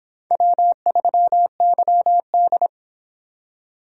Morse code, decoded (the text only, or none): W3YD